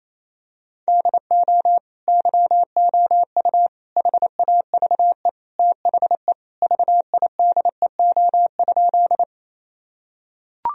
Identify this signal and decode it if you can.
Morse code, heard as a keyed tone, about 28 wpm, 700 Hz. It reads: DO YOU HAVE THE VIDEO?
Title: Sentences from Top 1000 Words - Review Part 4 - 28wpm